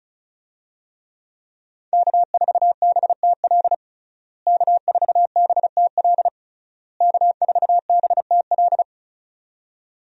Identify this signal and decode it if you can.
Morse code, heard as a keyed tone, about 36 wpm, 700 Hz. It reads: K4BTL K4BTL K4BTL
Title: USA Call Signs - 5 Characters - Repeated 3x - 35wpm